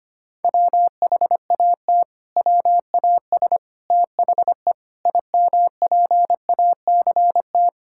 Morse code, heard as a keyed tone, about 25 wpm, 700 Hz. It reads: WHAT WAS THE IMPACT
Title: Sentences from Top 1000 Words - Review Part 6 - 25wpm